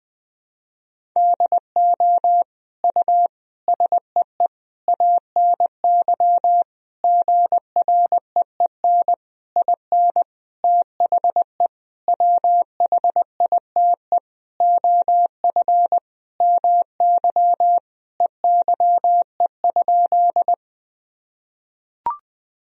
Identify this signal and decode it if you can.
Morse code, heard as a keyed tone, about 20 wpm, 700 Hz. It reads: DO U SEE ANY GREEN IN THE WHITE OF MY EYE?
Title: Sentences from Top 500 Words and Abbreviations 20wpm